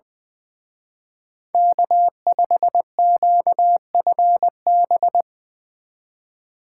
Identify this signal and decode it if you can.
Morse code, heard as a keyed tone, about 22 wpm, 700 Hz. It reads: K5QFB
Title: Speed Racing - USA Call Signs 20wpm